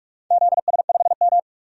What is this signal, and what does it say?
Morse code, heard as a keyed tone, about 42 wpm, 700 Hz. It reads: ZS5M
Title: DXCC-100 Call Signs - 4 Characters - 45wpm